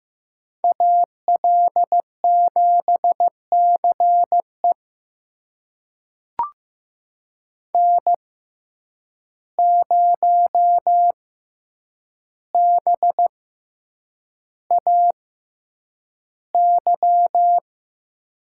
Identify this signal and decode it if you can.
Morse code, heard as a keyed tone, about 15 wpm, spaced as for 8 wpm, 700 Hz. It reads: AL7CE N0BAY
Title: USA Call Signs - 5 Characters - 6x Character Spacing - 15wpm